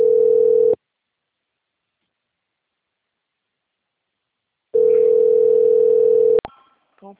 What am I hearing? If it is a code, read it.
Ringback tone (North America), 440 + 480 Hz.
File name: out- - - - - .12225